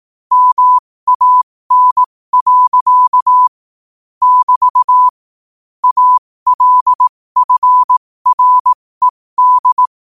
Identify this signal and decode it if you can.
Morse code, heard as a keyed tone, about 18 wpm, spaced as for 15 wpm, 1000 Hz. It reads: MAN. <BT> ALFRED
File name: qotdfiles / 15wpm